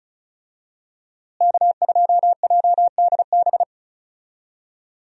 Morse code, heard as a keyed tone, about 36 wpm, 700 Hz. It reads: K2JDB
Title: USA Call Signs - 5 Characters - Repeated 3x - 35wpm